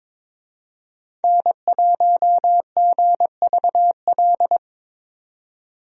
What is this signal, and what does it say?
Morse code, heard as a keyed tone, about 22 wpm, 700 Hz. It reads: N1GVL
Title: USA Call Signs - 5 Characters - 22wpm